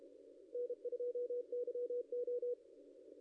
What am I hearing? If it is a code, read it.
Morse code, heard as a keyed tone, about 28 wpm, 500 Hz. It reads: N2YO
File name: contests / ss-cw-k5tr-op-n5rz -0700